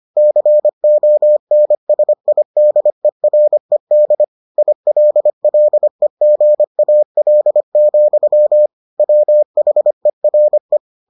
Morse code, heard as a keyed tone, about 25 wpm, 600 Hz. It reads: CONSIDERED ILLEGAL, WHERE